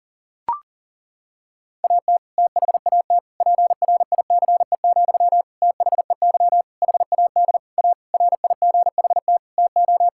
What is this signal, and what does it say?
Morse code, heard as a keyed tone, about 40 wpm, 700 Hz. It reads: AT THAT PRICE, THEY HAD A RIGHT TO
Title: Sentences from Top 1000 Words - Review Part 5 - 40wpm